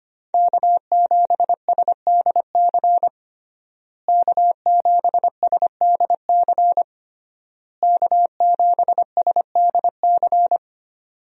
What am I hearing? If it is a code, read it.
Morse code, heard as a keyed tone, about 25 wpm, 700 Hz. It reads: K7SDC K7SDC K7SDC
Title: USA Call Signs - 5 Characters - Repeated 3x - 25wpm